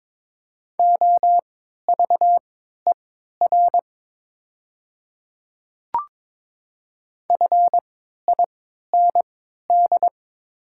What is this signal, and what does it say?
Morse code, heard as a keyed tone, about 22 wpm, spaced as for 12 wpm, 700 Hz. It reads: OVER FIND
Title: 3x Character Spacing - Top 100 Words - 22wpm